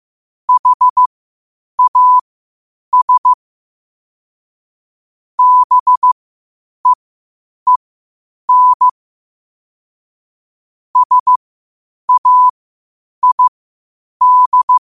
Morse code, heard as a keyed tone, about 15 wpm, spaced as for 8 wpm, 1000 Hz. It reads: HAS BEEN SAID
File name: qotdfiles / 8wpm